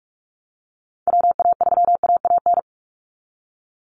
Morse code, heard as a keyed tone, about 44 wpm, 700 Hz. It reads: WA3AAN
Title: Call Signs Repeated 2x - 45wpm